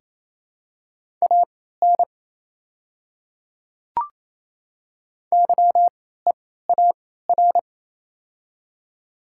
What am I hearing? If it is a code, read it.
Morse code, heard as a keyed tone, about 28 wpm, spaced as for 16 wpm, 700 Hz. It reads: AN YEAR
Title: 3x Character Spacing - Top 200 Words - 28wpm